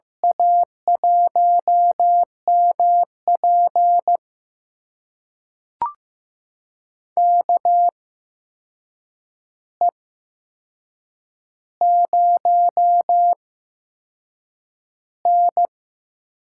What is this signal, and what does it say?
Morse code, heard as a keyed tone, about 15 wpm, spaced as for 6 wpm, 700 Hz. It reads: A1MP KE0N
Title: USA Call Signs - 8x Character Spacing - 15wpm